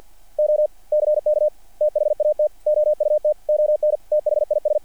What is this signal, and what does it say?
Morse code, heard as a keyed tone, about 50 wpm, 600 Hz. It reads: OCK THAT OUT ON THIS
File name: archived / CRSnet